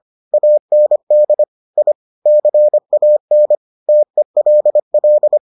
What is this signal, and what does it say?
Morse code, heard as a keyed tone, about 25 wpm, 600 Hz. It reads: AND I CAN TELL